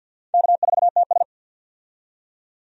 Morse code, heard as a keyed tone, about 47 wpm, 700 Hz. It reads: K4TS
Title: USA Call Signs - 4 Characters - 50wpm